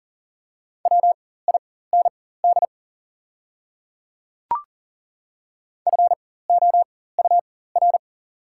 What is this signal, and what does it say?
Morse code, heard as a keyed tone, about 40 wpm, spaced as for 19 wpm, 690 Hz. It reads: WIND FOUR
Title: 4x Character Spacing - Top 500 Words - 40wpm